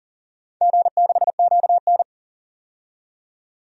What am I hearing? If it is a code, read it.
Morse code, heard as a keyed tone, about 38 wpm, 700 Hz. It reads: G6QN
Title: DXCC-100 Call Signs - 4 Characters - 40wpm